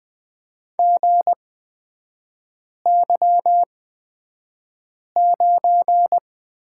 Morse code, heard as a keyed tone, about 20 wpm, spaced as for 5 wpm, 700 Hz. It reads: GY9
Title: 250 Lesson 40e - BK - Three Characters 20wpm